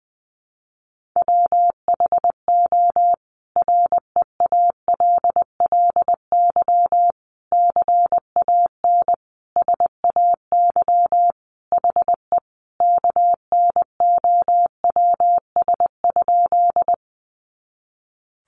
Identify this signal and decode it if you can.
Morse code, heard as a keyed tone, about 20 wpm, 700 Hz. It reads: WHO REALLY CAN SAY HE KNOWS?
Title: A Sound Of Thunder 20wpm